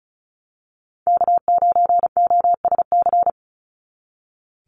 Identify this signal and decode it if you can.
Morse code, heard as a keyed tone, about 34 wpm, 700 Hz. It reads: K9OSC
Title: NA SOTA Chaser Callsigns 35wpm